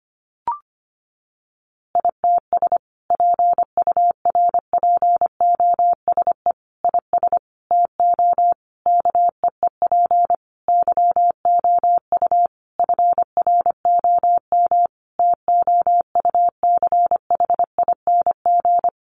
Morse code, heard as a keyed tone, about 25 wpm, 700 Hz. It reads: ITS PURPOSE IS TO KEEP YOU FROM TOUCHING
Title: A Sound Of Thunder 25wpm